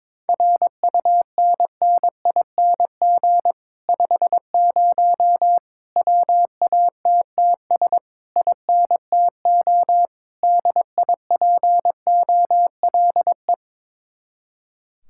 Morse code, heard as a keyed tone, about 22 wpm, 700 Hz. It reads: RUNNING 50 WATTS INTO DIPOLE
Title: Speed-Racing Common QSO Phrases - 22wpm